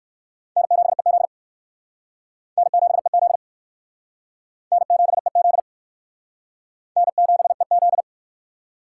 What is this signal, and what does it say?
Morse code, heard as a keyed tone, about 58 wpm, 700 Hz. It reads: N7EZ N7EZ N7EZ N7EZ